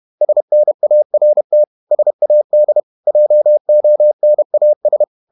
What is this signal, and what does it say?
Morse code, heard as a keyed tone, about 31 wpm, 600 Hz. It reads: SNART SAD JONAS
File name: Ferie QRQ 31wpm